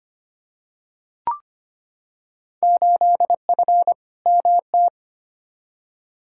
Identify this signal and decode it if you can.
Morse code, heard as a keyed tone, about 25 wpm, 700 Hz. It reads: 8F MT
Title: Field Day Exchanges Encore - 25wpm